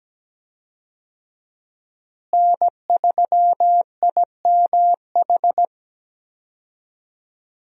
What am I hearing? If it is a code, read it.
Morse code, heard as a keyed tone, about 17 wpm, 700 Hz. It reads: N3IMH
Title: USA Call Signs - 5 Characters - Repeated 2x - 17wpm